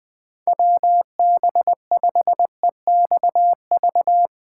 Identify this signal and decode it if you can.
Morse code, heard as a keyed tone, about 20 wpm, 700 Hz. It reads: WB5EXV
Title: Speed Racing - USA 2x3 Call Signs 17wpm